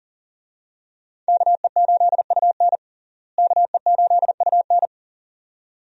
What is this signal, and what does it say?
Morse code, heard as a keyed tone, about 40 wpm, 700 Hz. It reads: KE8UN KE8UN
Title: USA Call Signs - 5 Characters - Repeated 2x - 40wpm